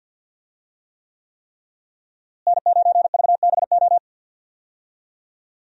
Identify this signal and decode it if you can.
Morse code, heard as a keyed tone, about 52 wpm, 700 Hz. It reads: N9VBO
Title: USA Call Signs - 5 Characters - Repeated 3x - 50wpm